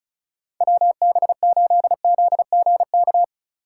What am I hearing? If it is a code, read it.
Morse code, heard as a keyed tone, about 44 wpm, 700 Hz. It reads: WB8ZGK